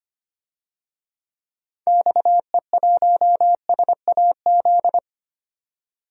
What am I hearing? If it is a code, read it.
Morse code, heard as a keyed tone, about 25 wpm, 700 Hz. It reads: XE1SAZ
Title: DXCC-100 Call Signs - 6 Characters - 25wpm